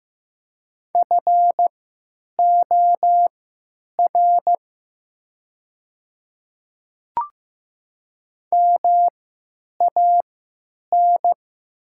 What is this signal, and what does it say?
Morse code, heard as a keyed tone, about 15 wpm, spaced as for 9 wpm, 700 Hz. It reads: FOR MAN
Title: 3x Character Spacing - Top 300 Words - 15wpm